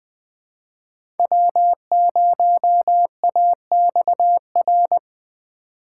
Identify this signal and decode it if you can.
Morse code, heard as a keyed tone, about 20 wpm, 700 Hz. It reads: W0AXR
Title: USA Call Signs - 5 Characters - 3x Character Spacing - 20wpm